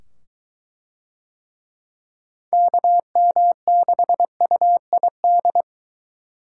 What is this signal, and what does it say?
Morse code, heard as a keyed tone, about 23 wpm, 700 Hz. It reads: KM6UID